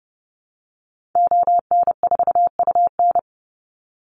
Morse code, heard as a keyed tone, about 30 wpm, 700 Hz. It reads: ON4UN
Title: Call Signs - 30wpm